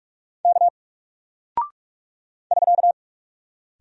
Morse code, heard as a keyed tone, about 44 wpm, 700 Hz. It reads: K <SK>
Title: Single Character (Letter:Number:Punctuation:Prosign) Alt 45wpm